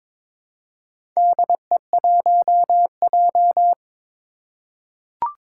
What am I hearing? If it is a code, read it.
Morse code, heard as a keyed tone, about 23 wpm, 700 Hz. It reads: DE1J
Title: Four Challenging Letter-Number Combinations 22wpm